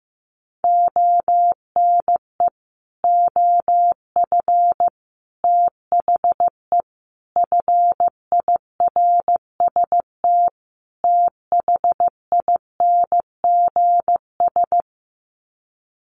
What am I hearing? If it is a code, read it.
Morse code, heard as a keyed tone, about 15 wpm, 700 Hz. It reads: ONE OF THE FIRST THINGS
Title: Sets of 5 Words Encore 15wpm